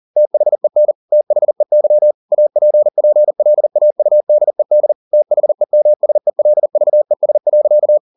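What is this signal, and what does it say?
Morse code, heard as a keyed tone, about 40 wpm, 600 Hz. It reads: THEN THEY APPLAUDED THEMSELVES.
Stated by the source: Morse code at 40 wpm, practice text, CW med Gnister.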